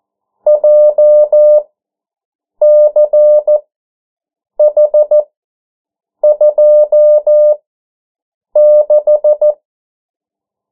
Morse code, heard as a keyed tone, about 14 wpm, spaced as for 7 wpm, 600 Hz. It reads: JCH26
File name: morse mp3 / 09 urok 8